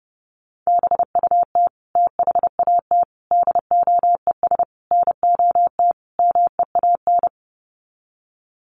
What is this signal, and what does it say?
Morse code, heard as a keyed tone, about 28 wpm, 700 Hz. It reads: BUT THAT DOES NOT MEAN
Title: Sets of 5 Words 30wpm